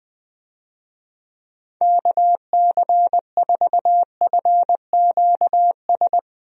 Morse code, heard as a keyed tone, about 21 wpm, 700 Hz. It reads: KC4FQS
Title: USA Call Signs - 6 Characters - Repeated 3x - 20wpm